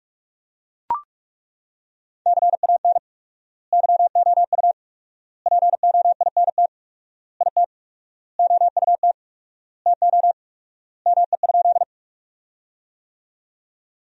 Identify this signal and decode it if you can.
Morse code, heard as a keyed tone, about 42 wpm, 700 Hz. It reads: CAN YOU POINT IT OUT TO ME?
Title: 4x Word-Spacing - Sentences from Top 200 Words - 45wpm